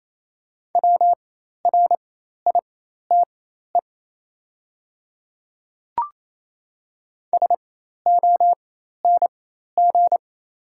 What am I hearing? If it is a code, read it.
Morse code, heard as a keyed tone, about 28 wpm, spaced as for 13 wpm, 690 Hz. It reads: WRITE SONG
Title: 4x Character Spacing - Top 500 Words - 28wpm